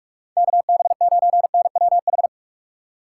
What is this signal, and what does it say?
Morse code, heard as a keyed tone, about 56 wpm, 700 Hz. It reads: KB9NWH